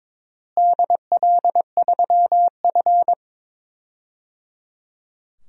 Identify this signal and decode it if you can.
Morse code, heard as a keyed tone, about 22 wpm, 700 Hz. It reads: DL3F
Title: DXCC-100 Call Signs - 4 Characters - Repeated 3x - 22wpm